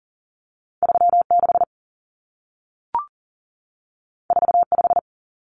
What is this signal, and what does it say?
Morse code, heard as a keyed tone, about 40 wpm, 700 Hz. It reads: 36 45
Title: Numbers - 2-Digit 40wpm